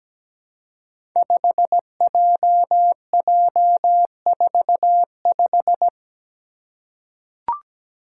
Morse code, heard as a keyed tone, about 17 wpm, 700 Hz. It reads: 5JJ45
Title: Five Challenging Letter-Number Combinations 17wpm